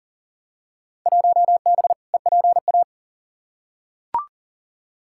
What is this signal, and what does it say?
Morse code, heard as a keyed tone, about 40 wpm, 700 Hz. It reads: 1B EPA
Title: Field Day Exchanges - 40wpm